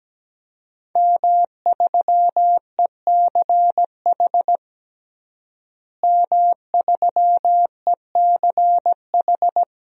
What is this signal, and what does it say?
Morse code, heard as a keyed tone, about 17 wpm, 700 Hz. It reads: M3ECH M3ECH
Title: DXCC-100 Call Signs - 5 Characters - Repeated 2x - 17wpm